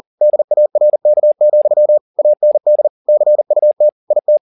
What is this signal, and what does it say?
Morse code, heard as a keyed tone, about 40 wpm, 600 Hz. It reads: DARK, AND CUT IT